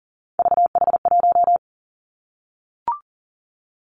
Morse code, heard as a keyed tone, about 40 wpm, 700 Hz. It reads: VH1
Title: Three Challenging Letter-Number Combinations 40wpm